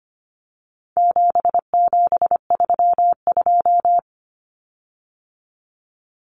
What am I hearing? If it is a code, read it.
Morse code, heard as a keyed tone, about 25 wpm, 700 Hz. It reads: 7732
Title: Common Cut Numbers - 4-Digit 25wpm